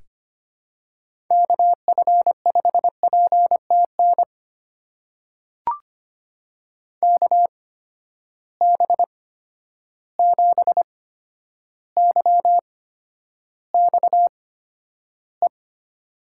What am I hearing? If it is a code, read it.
Morse code, heard as a keyed tone, about 25 wpm, spaced as for 11 wpm, 700 Hz. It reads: KF5PTN KB7YXE
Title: USA Call Signs - 6 Characters - 8x Character Spacing - 25wpm